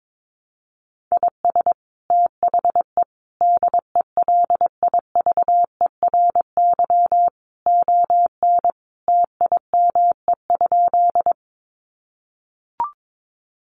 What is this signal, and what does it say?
Morse code, heard as a keyed tone, about 22 wpm, 700 Hz. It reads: IS THE DELIVERY ON TIME?
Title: Sentences from Top 1500 Words 22wpm